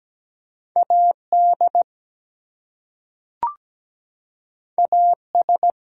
Morse code, heard as a keyed tone, about 17 wpm, 700 Hz. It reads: AD AS
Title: Two Letter Words - 17wpm